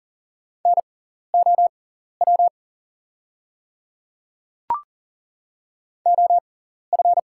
Morse code, heard as a keyed tone, about 40 wpm, spaced as for 14 wpm, 700 Hz. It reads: NOW OF
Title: 6x Character Spacing - Top 100 Words - 40wpm